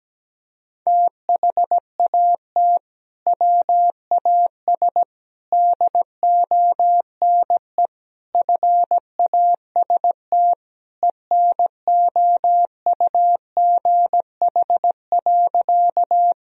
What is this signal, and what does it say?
Morse code, heard as a keyed tone, about 17 wpm, 700 Hz. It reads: THAT WAS DONE FAST ENOUGH.
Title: Sentences from Top 1000 Words - Review Part 2 - 17wpm